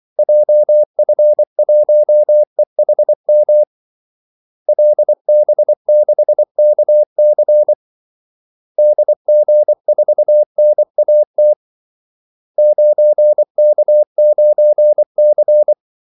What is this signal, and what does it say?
Morse code, heard as a keyed tone, about 24 wpm, 600 Hz. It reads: JF1EHM LB6KC DG4NAT 9K9C
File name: Callsigns-24wpm